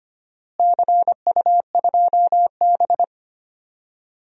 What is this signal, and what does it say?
Morse code, heard as a keyed tone, about 25 wpm, 700 Hz. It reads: CU2B
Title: DXCC-100 Call Signs - 4 Characters - Repeated 3x - 25wpm